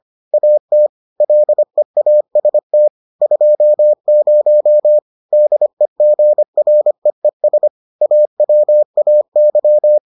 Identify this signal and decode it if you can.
Morse code, heard as a keyed tone, about 25 wpm, 600 Hz. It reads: AT LEAST 20 DEGREES AWAY